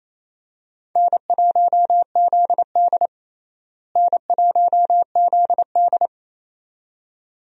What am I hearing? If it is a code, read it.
Morse code, heard as a keyed tone, about 28 wpm, 700 Hz. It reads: N1ZD N1ZD
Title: USA Call Signs - 4 Characters - Repeated 2x - 28wpm